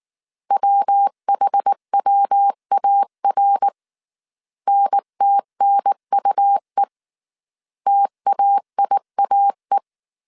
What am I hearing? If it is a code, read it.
Morse code, heard as a keyed tone, about 19 wpm, 800 Hz. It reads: WHWAR NTNUE TAIAE